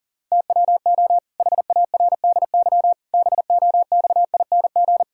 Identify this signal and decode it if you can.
Morse code, heard as a keyed tone, about 40 wpm, 700 Hz. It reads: TWO HARDY BOXING
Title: Pangram Challenge v2 - 40wpm